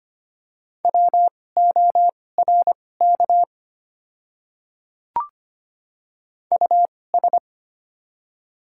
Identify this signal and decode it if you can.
Morse code, heard as a keyed tone, about 25 wpm, spaced as for 18 wpm, 700 Hz. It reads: WORK US
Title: 2x Character Spacing - Top 200 Words - 25wpm